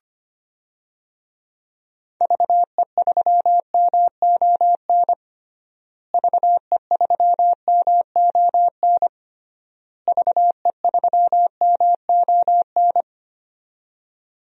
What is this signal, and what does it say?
Morse code, heard as a keyed tone, about 25 wpm, 700 Hz. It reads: VE3MON VE3MON VE3MON
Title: DXCC-100 Call Signs - 6 Characters - Repeated 3x - 25wpm